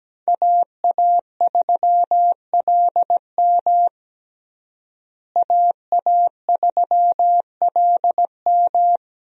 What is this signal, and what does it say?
Morse code, heard as a keyed tone, about 17 wpm, 700 Hz. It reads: AA3LM AA3LM
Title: USA Call Signs - 5 Characters - Repeated 2x - 17wpm